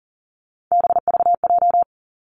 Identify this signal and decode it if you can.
Morse code, heard as a keyed tone, about 39 wpm, 700 Hz. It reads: BVJ
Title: Three Challenging Letter-Number Combinations 40wpm